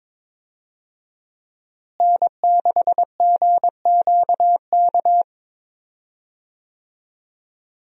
Morse code, heard as a keyed tone, about 22 wpm, 700 Hz. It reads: N6GQK
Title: USA Call Signs - 5 Characters - Repeated 2x - 22wpm